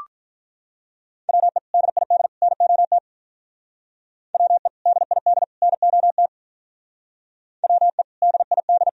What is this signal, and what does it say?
Morse code, heard as a keyed tone, about 46 wpm, 700 Hz. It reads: WE DID NOT WE DID NOT WE DID